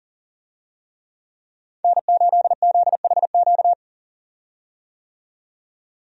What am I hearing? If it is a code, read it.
Morse code, heard as a keyed tone, about 40 wpm, 700 Hz. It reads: N8ZHQ
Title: USA Call Signs - 5 Characters - 40wpm